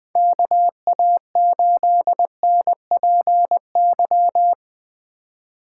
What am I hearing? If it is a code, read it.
Morse code, heard as a keyed tone, about 20 wpm, 700 Hz. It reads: KA8NPY
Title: USA Call Signs - 4x Character Spacing - 20wpm